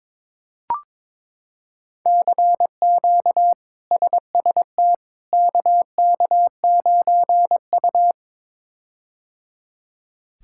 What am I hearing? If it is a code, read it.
Morse code, heard as a keyed tone, about 22 wpm, 700 Hz. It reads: CQ SST KK9U
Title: K1USN SST QSO Practice - 22wpm